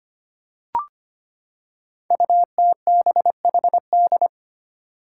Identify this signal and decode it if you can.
Morse code, heard as a keyed tone, about 25 wpm, 700 Hz. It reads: UTBHD
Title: Five Challenging Letter-Number Combinations 25wpm